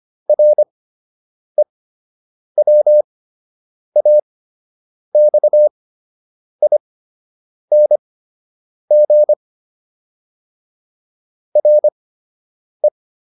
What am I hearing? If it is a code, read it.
Morse code, heard as a keyed tone, about 25 wpm, spaced as for 8 wpm, 600 Hz. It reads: REWAXING RE